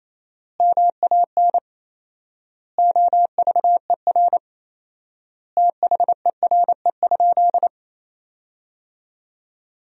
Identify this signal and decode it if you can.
Morse code, heard as a keyed tone, about 28 wpm, 700 Hz. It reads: MAN OVER THERE?
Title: 4x Word-Spacing - Sentences from Top 200 Words - 28wpm